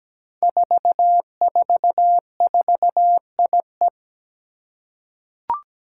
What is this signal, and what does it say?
Morse code, heard as a keyed tone, about 17 wpm, 700 Hz. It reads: 444IE
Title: Five Challenging Letter-Number Combinations 17wpm